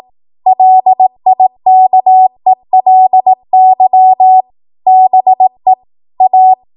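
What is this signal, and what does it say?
Morse code, heard as a keyed tone, about 18 wpm, 750 Hz. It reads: LIKELY BE A
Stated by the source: ARRL 18 wpm Morse code practice archive transmission.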